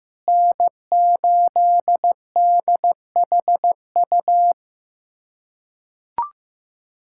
Morse code, heard as a keyed tone, about 15 wpm, 700 Hz. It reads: N8DHU
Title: USA Callsign DE Callsign - 5 Characters - 15wpm